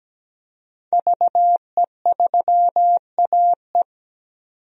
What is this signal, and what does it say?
Morse code, heard as a keyed tone, about 17 wpm, 700 Hz. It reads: VE3AE
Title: DXCC-100 Call Signs - 5 Characters - 17wpm